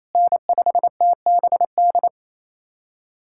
Morse code, heard as a keyed tone, about 28 wpm, 700 Hz. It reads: N5TBD
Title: USA Callsign DE Callsign - 5 Characters - 28wpm